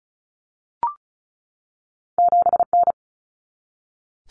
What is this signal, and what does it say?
Morse code, heard as a keyed tone, about 36 wpm, 700 Hz. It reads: 7N
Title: Common Cut Numbers - 2-Digit 35wpm